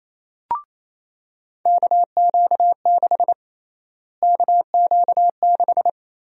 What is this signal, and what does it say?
Morse code, heard as a keyed tone, about 28 wpm, 700 Hz. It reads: KQ6 KQ6
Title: USA Call Sign Prefixes - Repeated 2x - 28wpm